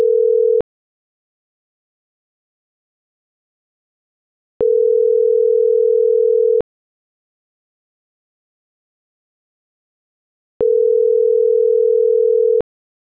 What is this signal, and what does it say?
Ringback tone (North America), 440 + 480 Hz.
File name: CENTRAL MAERSK / out isabel.bravo